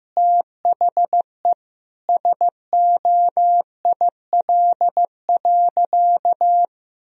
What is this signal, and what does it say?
Morse code, heard as a keyed tone, about 15 wpm, 700 Hz. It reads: THE SOIL.
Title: Sentences from Top 1000 Words - Review Part 4 - 15wpm